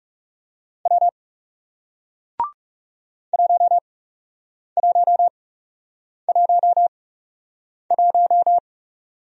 Morse code, heard as a keyed tone, about 39 wpm, spaced as for 8 wpm, 700 Hz. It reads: W 1111